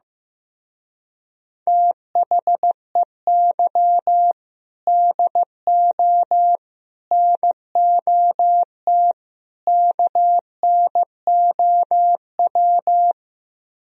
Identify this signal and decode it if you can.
Morse code, heard as a keyed tone, about 15 wpm, 700 Hz. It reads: THEY DO NOT KNOW
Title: Speed-Racing - Sets of 4 Words 15wpm